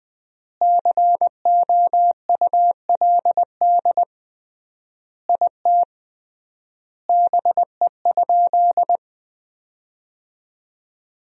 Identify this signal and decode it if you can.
Morse code, heard as a keyed tone, about 21 wpm, 700 Hz. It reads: COULD IT BE?
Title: 3x Word-Spacing - Sentences from Top 100 Words - 20wpm